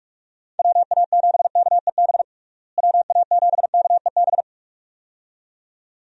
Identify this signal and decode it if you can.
Morse code, heard as a keyed tone, about 46 wpm, 700 Hz. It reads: WA7KEB WA7KEB
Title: USA Call Signs - 6 Characters - Repeated 2x - 45wpm